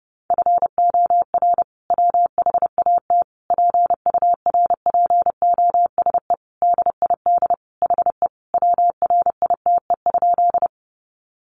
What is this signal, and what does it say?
Morse code, heard as a keyed tone, about 30 wpm, 700 Hz. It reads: FOR WHAT PURPOSE DID HE WRITE?
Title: Sentences from Top 800 Words 30wpm